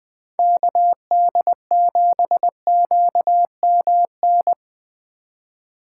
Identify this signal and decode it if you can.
Morse code, heard as a keyed tone, about 24 wpm, 700 Hz. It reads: KD7QMN
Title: Speed Racing - USA 2x3 Call Signs 20wpm